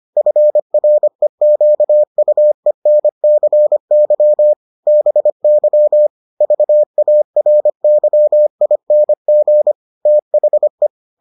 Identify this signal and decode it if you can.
Morse code, heard as a keyed tone, about 25 wpm, 600 Hz. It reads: FREQUENCY BY VARYING THE